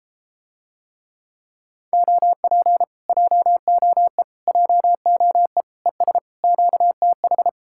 Morse code, heard as a keyed tone, about 33 wpm, 700 Hz. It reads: OP JOE JOE ES QTH